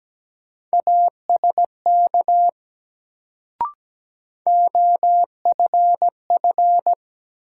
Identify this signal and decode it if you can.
Morse code, heard as a keyed tone, about 17 wpm, 700 Hz. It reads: ASK OFF
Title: Sentences from Top 200 Words - New Word Drill - 17wpm